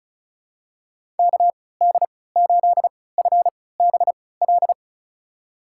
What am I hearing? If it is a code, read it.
Morse code, heard as a keyed tone, about 36 wpm, spaced as for 20 wpm, 700 Hz. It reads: KD8FBL
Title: USA Call Signs - 6 Characters - 3x Character Spacing - 35wpm